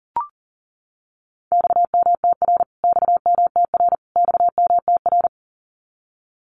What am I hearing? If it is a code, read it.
Morse code, heard as a keyed tone, about 40 wpm, 700 Hz. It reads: XMTR XMTR XMTR
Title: Abbreviations Repeated 3x - 40wpm